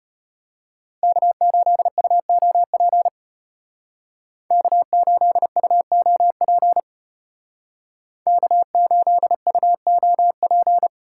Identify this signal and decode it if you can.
Morse code, heard as a keyed tone, about 32 wpm, 700 Hz. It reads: K8UOP K8UOP K8UOP